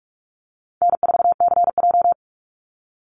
Morse code, heard as a keyed tone, about 46 wpm, 700 Hz. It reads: N4CJ
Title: Easy Call Signs - 45wpm